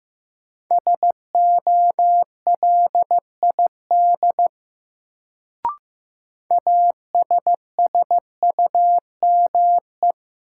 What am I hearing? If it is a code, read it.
Morse code, heard as a keyed tone, about 15 wpm, 700 Hz. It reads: SOLID ASSUME
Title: Sentences from Top 1500 Words - New Word Drill - 15wpm